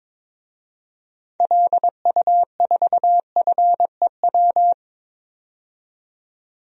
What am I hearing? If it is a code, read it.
Morse code, heard as a keyed tone, about 22 wpm, 700 Hz. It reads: LU4FEW
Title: DXCC-100 Call Signs - 6 Characters - 22wpm